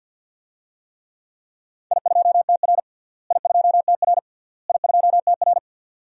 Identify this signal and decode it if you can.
Morse code, heard as a keyed tone, about 49 wpm, 700 Hz. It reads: I2TR I2TR I2TR
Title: DXCC-100 Call Signs - 4 Characters - Repeated 3x - 50wpm